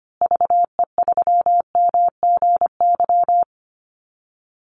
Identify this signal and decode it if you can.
Morse code, heard as a keyed tone, about 25 wpm, 700 Hz. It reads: VE3MGY
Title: Call Signs Repeated 3x - 25wpm